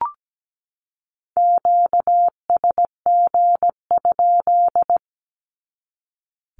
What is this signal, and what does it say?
Morse code, heard as a keyed tone, about 17 wpm, 700 Hz. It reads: QSG?
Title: Q-Codes 17wpm